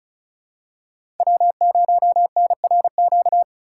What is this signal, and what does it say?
Morse code, heard as a keyed tone, about 36 wpm, 700 Hz. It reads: W0NRQ
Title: USA Call Signs - 5 Characters - 2x Character Spacing - 35wpm